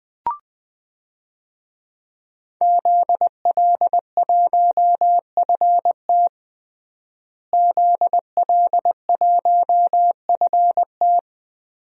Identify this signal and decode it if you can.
Morse code, heard as a keyed tone, about 20 wpm, 700 Hz. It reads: ZL1FT ZL1FT
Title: DXCC-100 Call Signs - 5 Characters - Repeated 2x - 20wpm